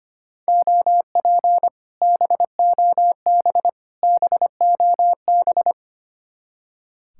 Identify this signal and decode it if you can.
Morse code, heard as a keyed tone, about 32 wpm, 700 Hz. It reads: OP BOB BOB